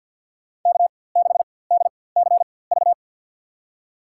Morse code, heard as a keyed tone, about 48 wpm, spaced as for 23 wpm, 700 Hz. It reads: K6DCV
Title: USA Call Signs - 5 Characters - 4x Character Spacing - 50wpm